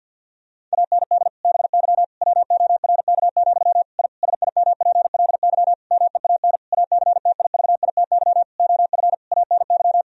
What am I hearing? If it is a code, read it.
Morse code, heard as a keyed tone, about 50 wpm, 700 Hz. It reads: AND BY WORK, I SIMPLY MEAN ACTIVITY OF ANY
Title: Sentences from Top 1000 Words - Review Part 4 - 50wpm